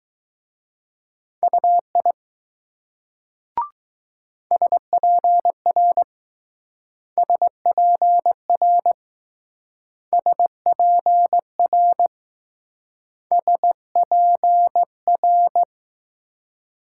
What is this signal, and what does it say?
Morse code, heard as a keyed tone, about 19 wpm, 700 Hz. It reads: UI SPR SPR SPR SPR